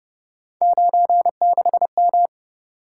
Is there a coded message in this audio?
Yes: Morse code from 0.6 to 2.3 s, heard as a keyed tone, about 28 wpm, 700 Hz. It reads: 96M